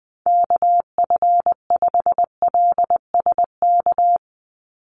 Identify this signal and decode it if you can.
Morse code, heard as a keyed tone, about 20 wpm, 700 Hz. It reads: KF5LSK
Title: USA Call Signs 20wpm